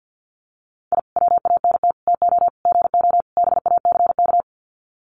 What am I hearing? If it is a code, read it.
Morse code, heard as a keyed tone, about 50 wpm, 700 Hz. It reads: I WANT TO GO BACK
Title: Sets of 5 Words Encore 50wpm